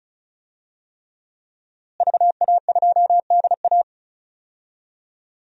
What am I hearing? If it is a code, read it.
Morse code, heard as a keyed tone, about 36 wpm, 700 Hz. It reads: VA2DA
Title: DXCC-100 Call Signs - 5 Characters - Repeated 3x - 35wpm